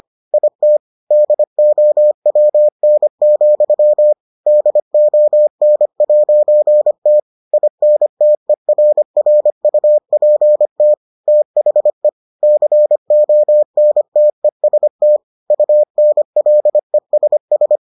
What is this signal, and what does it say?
Morse code, heard as a keyed tone, about 25 wpm, 600 Hz. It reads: IT DOWN, DON'T INTERRUPT THE CONTEST UNLESS